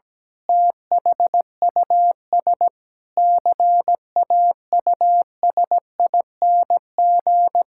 Morse code, heard as a keyed tone, about 17 wpm, 700 Hz. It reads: THUS CAUSING
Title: Pangram Challenge v2 - 17wpm